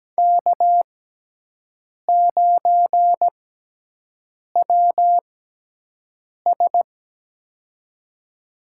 Morse code, heard as a keyed tone, about 17 wpm, spaced as for 6 wpm, 700 Hz. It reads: K9WS
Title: USA Call Signs - 6x Character Spacing - 17wpm